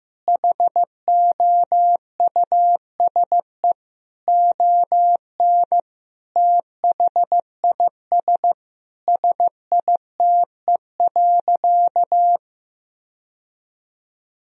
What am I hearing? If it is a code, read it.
Morse code, heard as a keyed tone, about 15 wpm, 700 Hz. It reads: HOUSE ON THIS SITE.
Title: Sentences from Top 1000 Words - Review Part 4 - 15wpm